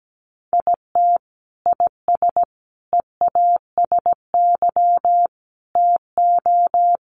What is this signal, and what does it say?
Morse code, heard as a keyed tone, about 17 wpm, 700 Hz. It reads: IT IS EASY TO
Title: Sentences from Top 1100 Words 17wpm